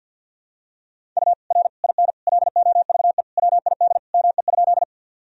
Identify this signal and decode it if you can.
Morse code, heard as a keyed tone, about 51 wpm, 700 Hz. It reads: U R IN LOVE WID ME?
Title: Sentences from Top 400 Words and Abbreviations 50wpm